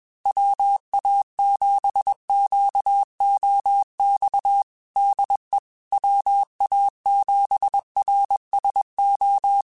Morse code, heard as a keyed tone, about 21 wpm, 800 Hz. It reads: WA7QOX DE WA7RSO
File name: QSOs / QSOs-mp3s / WA7QOX 2@20